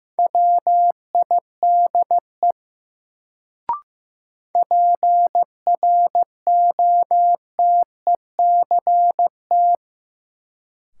Morse code, heard as a keyed tone, about 15 wpm, 700 Hz. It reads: WIDE PROTECT
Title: Sentences from Top 900 Words - New Word Drill - 15wpm